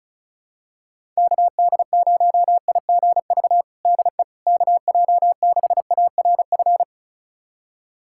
Morse code, heard as a keyed tone, about 36 wpm, 690 Hz. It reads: KD0IGV DE KJ6ARF
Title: USA Callsign DE Callsign - 6 Characters - 35wpm